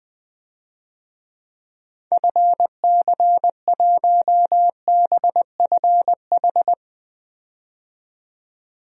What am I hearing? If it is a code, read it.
Morse code, heard as a keyed tone, about 20 wpm, 700 Hz. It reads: FC1BFH
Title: DXCC-100 Call Signs - 6 Characters - 20wpm